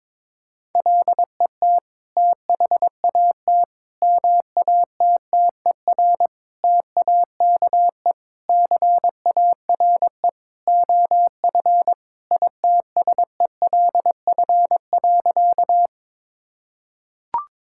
Morse code, heard as a keyed tone, about 22 wpm, 700 Hz. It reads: LET THAT MATTER TAKE CARE OF ITSELF.
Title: Sentences from Top 1000 Words - Review Part 5 - 22wpm